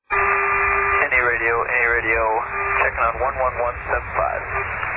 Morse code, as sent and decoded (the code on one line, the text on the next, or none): none